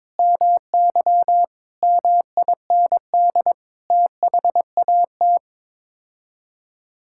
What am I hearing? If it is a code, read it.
Morse code, heard as a keyed tone, about 22 wpm, 700 Hz. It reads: MY MIND THAT
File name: practice / 22wpm / Sets of 8 Words - 22wpm